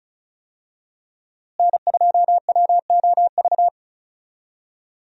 Morse code, heard as a keyed tone, about 36 wpm, 700 Hz. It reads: N2WOV
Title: USA Call Signs - 5 Characters - Repeated 2x - 35wpm